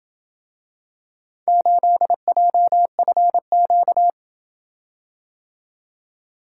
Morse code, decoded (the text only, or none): 8JFQ